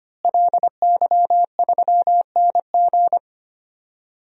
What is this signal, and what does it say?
Morse code, heard as a keyed tone, about 25 wpm, 700 Hz. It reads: LY3NG
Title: DXCC-100 Call Signs - 5 Characters - Repeated 2x - 25wpm